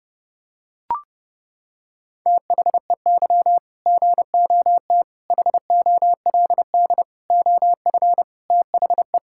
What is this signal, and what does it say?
Morse code, heard as a keyed tone, about 30 wpm, 700 Hz. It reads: THEY GOT HOLD OF THE
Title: Sentences from Top 1600 Words 30wpm